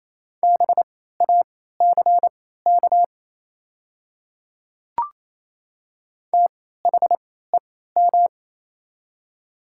Morse code, heard as a keyed tone, about 27 wpm, spaced as for 16 wpm, 700 Hz. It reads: BACK THEM
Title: 3x Character Spacing - Top 200 Words - 28wpm